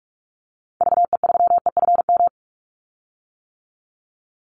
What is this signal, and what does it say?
Morse code, heard as a keyed tone, about 44 wpm, 700 Hz. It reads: VE3EFM
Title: Call Signs - 45wpm